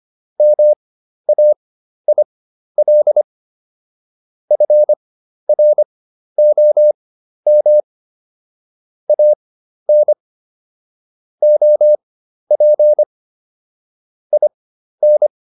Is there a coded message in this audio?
Yes: Morse code from 0.4 to 15.3 s, heard as a keyed tone, about 25 wpm, spaced as for 12 wpm, 600 Hz. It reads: MAIL FROM AN OP IN